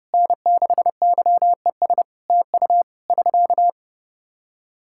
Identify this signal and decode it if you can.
Morse code, heard as a keyed tone, about 32 wpm, 700 Hz. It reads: N6YES TU <SK>
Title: Practice QSO - Exchange 4 - Ending - 30wpm